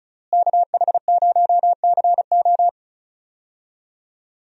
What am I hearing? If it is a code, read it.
Morse code, heard as a keyed tone, about 36 wpm, 700 Hz. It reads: KH0CO
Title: USA Call Signs - 5 Characters - Repeated 2x - 35wpm